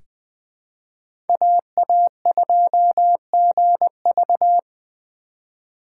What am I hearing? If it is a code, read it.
Morse code, heard as a keyed tone, about 20 wpm, 700 Hz. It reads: AA2GV
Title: USA Call Signs - 5 Characters - 2x Character Spacing - 20wpm